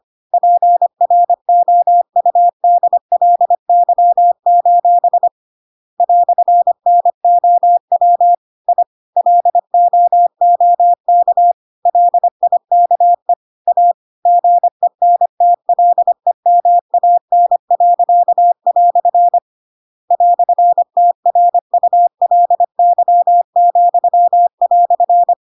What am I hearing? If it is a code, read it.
Morse code, heard as a keyed tone, about 25 wpm, 700 Hz. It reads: PROUDLY: "NOW I LOOK LIKE A GENTLEMAN." "TRULY,"